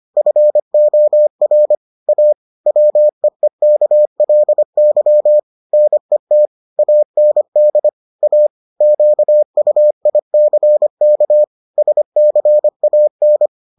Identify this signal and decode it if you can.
Morse code, heard as a keyed tone, about 25 wpm, 600 Hz. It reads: FOR A WEEKLY NET AND A QUICK SCAN